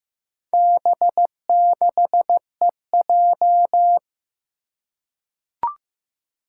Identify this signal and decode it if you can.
Morse code, heard as a keyed tone, about 15 wpm, 700 Hz. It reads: B6EJ
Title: Four Challenging Letter-Number Combinations 15wpm